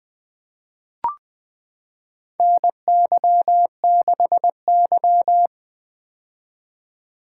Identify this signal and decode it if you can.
Morse code, heard as a keyed tone, about 20 wpm, 700 Hz. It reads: NY6Y
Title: K1USN SST QSO Practice - 20wpm